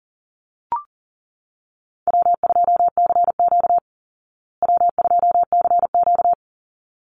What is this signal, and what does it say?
Morse code, heard as a keyed tone, about 40 wpm, 700 Hz. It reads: W2CQ W2CQ
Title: Call Signs Repeated 3x - 40wpm